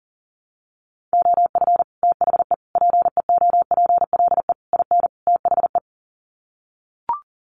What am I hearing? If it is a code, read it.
Morse code, heard as a keyed tone, about 40 wpm, 700 Hz. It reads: OF THE PEOPLE IN THE
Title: Sets of 5 Words Encore 40wpm